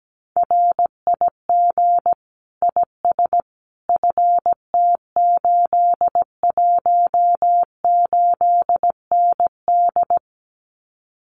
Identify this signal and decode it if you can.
Morse code, heard as a keyed tone, about 17 wpm, 700 Hz. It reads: RIG IS FT818ND
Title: Common QSO Phrases Encore 17wpm